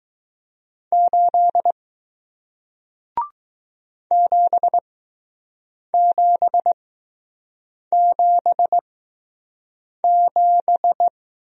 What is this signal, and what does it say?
Morse code, heard as a keyed tone, about 20 wpm, spaced as for 6 wpm, 700 Hz. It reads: 8 7777